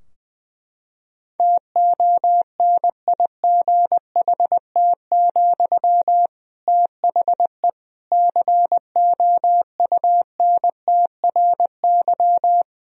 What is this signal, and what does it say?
Morse code, heard as a keyed tone, about 21 wpm, 700 Hz. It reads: TONIGHT, THE COUNTRY
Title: Sentences from Top 1000 Words - Review Part 4 - 20wpm